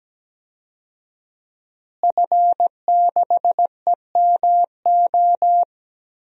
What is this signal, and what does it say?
Morse code, heard as a keyed tone, about 17 wpm, 700 Hz. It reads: F6EMO
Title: DXCC-100 Call Signs - 5 Characters - Repeated 3x - 17wpm